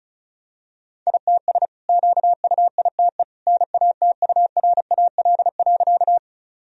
Morse code, heard as a keyed tone, about 36 wpm, 700 Hz. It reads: ITS QUITE NATURAL.
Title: Sentences from Top 1000 Words - Review Part 4 - 35wpm